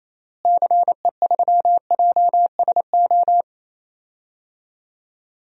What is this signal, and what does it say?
Morse code, heard as a keyed tone, about 28 wpm, 700 Hz. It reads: CE3JSO
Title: DXCC-100 Call Signs - 6 Characters - Repeated 2x - 28wpm